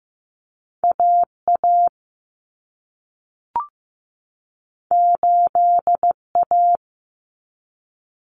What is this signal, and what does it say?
Morse code, heard as a keyed tone, about 15 wpm, 700 Hz. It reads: AA 8A
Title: Common Cut Numbers - 2-Digit 15wpm